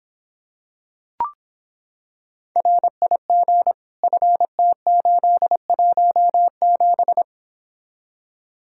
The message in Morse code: RIG FT817

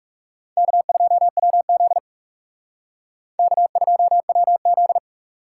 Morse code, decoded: K2WZ K2WZ

